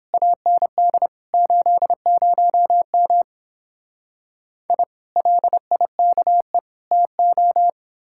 Morse code, heard as a keyed tone, about 29 wpm, 700 Hz. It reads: AND 80M I LIKE TO